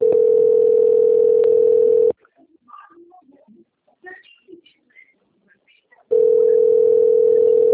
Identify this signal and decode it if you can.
Ringback tone (North America), 440 + 480 Hz.